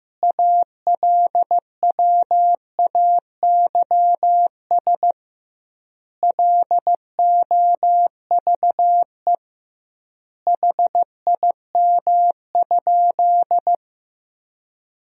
Morse code, heard as a keyed tone, about 15 wpm, 700 Hz. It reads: ALWAYS LOVE HIM?
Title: 2x Word-Spacing - Sentences from Top 400 Words - 15wpm